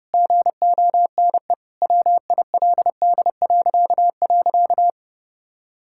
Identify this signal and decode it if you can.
Morse code, heard as a keyed tone, about 30 wpm, 700 Hz. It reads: GONE WILD..
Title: Pangram Challenge v2 - 30wpm